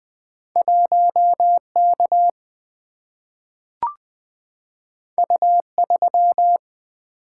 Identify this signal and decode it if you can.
Morse code, heard as a keyed tone, about 20 wpm, 700 Hz. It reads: 1K U3
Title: Single-Letter Single-Number 20wpm